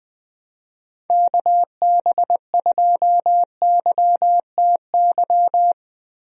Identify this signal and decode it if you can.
Morse code, heard as a keyed tone, about 20 wpm, 700 Hz. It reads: KB2YTY
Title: USA Call Signs - 6 Characters - Repeated 3x - 20wpm